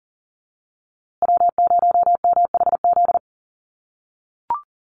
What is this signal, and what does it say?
Morse code, heard as a keyed tone, about 40 wpm, 700 Hz. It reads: W0MHZ
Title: USA Call Signs Repeated 3x 40wpm